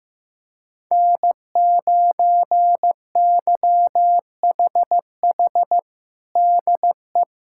Morse code, heard as a keyed tone, about 15 wpm, 700 Hz. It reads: N9YHH DE
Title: USA Callsign DE Callsign - 5 Characters - 15wpm